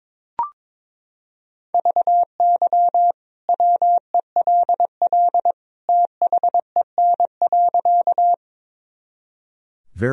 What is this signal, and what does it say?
Morse code, heard as a keyed tone, about 22 wpm, 700 Hz. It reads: VY WELL THEN.
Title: Sentences from Top 200 Words and Abbreviations 22wpm